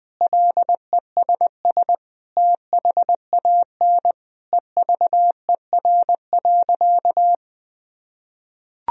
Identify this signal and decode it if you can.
Morse code, heard as a keyed tone, about 21 wpm, 700 Hz. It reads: LESS THAN EVER.
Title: Sentences from Top 1000 Words - Review Part 2 - 20wpm